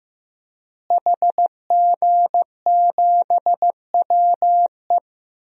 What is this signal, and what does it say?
Morse code, heard as a keyed tone, about 15 wpm, 700 Hz. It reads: HG7WE
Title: DXCC-100 Call Signs - 5 Characters - Repeated 2x - 15wpm